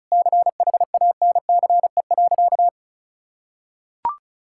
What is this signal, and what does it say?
Morse code, heard as a keyed tone, about 34 wpm, 700 Hz. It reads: CHANCE.
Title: Sentences from Top 1000 Words - Review Part 6 - 35wpm